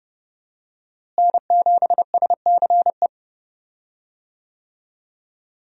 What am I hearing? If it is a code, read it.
Morse code, heard as a keyed tone, about 30 wpm, 700 Hz. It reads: N7SCE
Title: USA Call Signs - 5 Characters - Repeated 3x - 30wpm